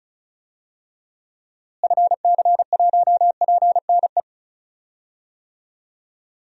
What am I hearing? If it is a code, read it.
Morse code, heard as a keyed tone, about 36 wpm, 700 Hz. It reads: FC1PNE
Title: DXCC-100 Call Signs - 6 Characters - Repeated 2x - 35wpm